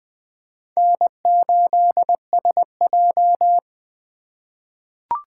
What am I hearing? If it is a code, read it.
Morse code, heard as a keyed tone, about 20 wpm, 700 Hz. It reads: N8SJ
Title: USA Call Signs - 4 Characters - 3x Character Spacing - 20wpm